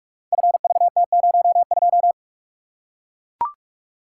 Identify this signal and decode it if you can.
Morse code, heard as a keyed tone, about 44 wpm, 700 Hz. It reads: FVT02
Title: Code Groups - 5 Letter:Numbers - 45wpm